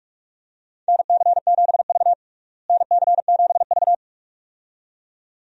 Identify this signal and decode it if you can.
Morse code, heard as a keyed tone, about 46 wpm, 700 Hz. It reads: NC7V NC7V
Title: USA Call Signs - 4 Characters - Repeated 2x - 45wpm